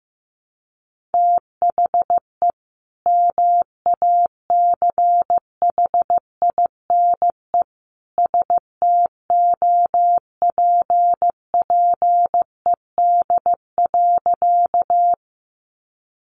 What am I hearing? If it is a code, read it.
Morse code, heard as a keyed tone, about 14 wpm, 700 Hz. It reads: THE MACHINE STOPPED.
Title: A Sound Of Thunder 15wpm